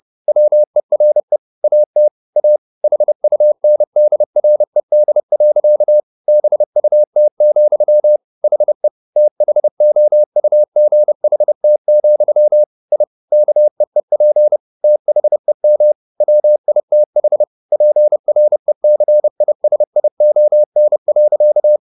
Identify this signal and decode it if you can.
Morse code, heard as a keyed tone, about 30 wpm, 600 Hz. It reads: WERE AT A HUNDRED. BUT, HE THOUGHT, I KEEP THEM WITH PRECISION.
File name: the old man and the sea / the old man and the sea 1 30wpm